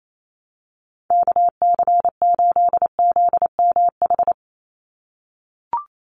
Morse code, heard as a keyed tone, about 28 wpm, 700 Hz. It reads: KC8ZMH
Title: USA Call Signs 28wpm